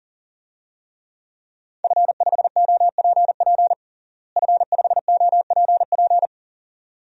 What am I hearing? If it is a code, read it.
Morse code, heard as a keyed tone, about 40 wpm, 690 Hz. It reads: F5OPP F5OPP
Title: DXCC-100 Call Signs - 5 Characters - Repeated 2x - 40wpm